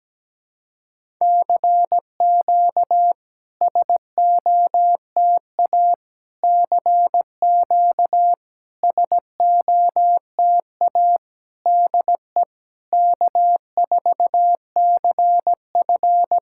Morse code, heard as a keyed tone, about 17 wpm, 700 Hz. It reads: CQ SOTA CQ SOTA DE K4CF
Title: SOTA QSO Practice 17wpm